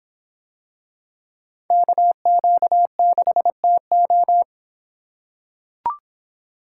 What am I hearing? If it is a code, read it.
Morse code, heard as a keyed tone, about 26 wpm, 700 Hz. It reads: KQ6TO